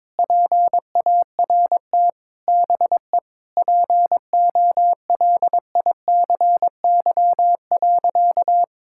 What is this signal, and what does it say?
Morse code, heard as a keyed tone, about 22 wpm, 700 Hz. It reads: PART BE POLICY.
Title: Sentences from Top 1000 Words - Review Part 5 - 22wpm